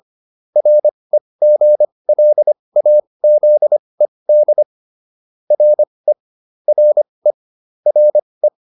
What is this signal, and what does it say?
Morse code, heard as a keyed tone, about 25 wpm, spaced as for 20 wpm, 600 Hz. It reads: REGLAZED RE RE RE